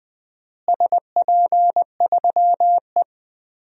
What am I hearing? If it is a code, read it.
Morse code, heard as a keyed tone, about 21 wpm, 700 Hz. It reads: SP3E
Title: DXCC-100 Call Signs - 4 Characters - Repeated 3x - 20wpm